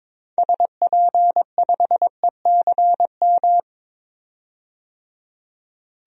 Morse code, heard as a keyed tone, about 22 wpm, 700 Hz. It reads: SP5ECM
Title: DXCC-100 Call Signs - 6 Characters - 22wpm